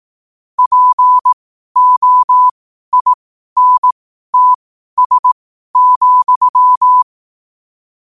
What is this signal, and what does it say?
Morse code, heard as a keyed tone, about 18 wpm, spaced as for 12 wpm, 1000 Hz. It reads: POINTS,
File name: qotdfiles / 12wpm